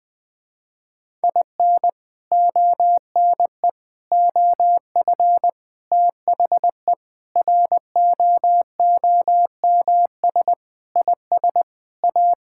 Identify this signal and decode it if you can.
Morse code, heard as a keyed tone, about 21 wpm, 700 Hz. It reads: IN ONE OF THE ROOMS IS A
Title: Sentences with 4-Word Phrases - Spoken Beginning - 20wpm